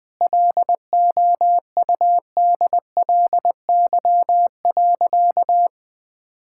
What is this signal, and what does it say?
Morse code, heard as a keyed tone, about 21 wpm, 700 Hz. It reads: LOUDLY.
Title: Easy Sentences with Hints - 20wpm